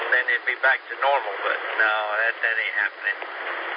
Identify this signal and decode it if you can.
Emergency Alert System (SAME) tones, heard as harsh buzzing bursts, with speech. It is heard from 0.4 to 3.0 s.